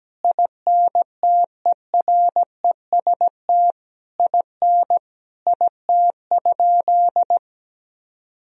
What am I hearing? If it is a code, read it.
Morse code, heard as a keyed tone, about 17 wpm, 700 Hz. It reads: INTEREST IN IT?
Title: Sentences from Top 1000 Words - Review Part 4 - 17wpm